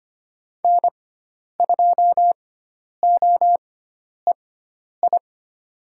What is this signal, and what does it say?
Morse code, heard as a keyed tone, about 25 wpm, spaced as for 10 wpm, 700 Hz. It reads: N2OEI
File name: Lesson 16g - Letter c - Call Signs 25wpm